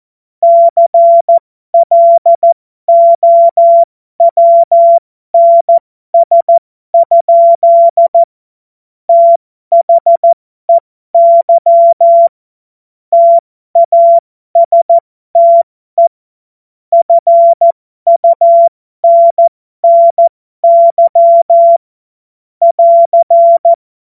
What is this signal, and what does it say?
Morse code, heard as a keyed tone, about 14 wpm, spaced as for 12 wpm, 680 Hz. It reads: CLOWNS? THEY TASTE FUNNY <AR>
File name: Morsekurs / Witze